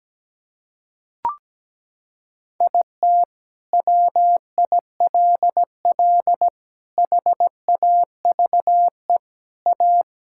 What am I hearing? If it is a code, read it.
Morse code, heard as a keyed tone, about 17 wpm, 700 Hz. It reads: IT WILL HAVE A
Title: Sentences from Top 1000 Words - Review Part 6 - 17wpm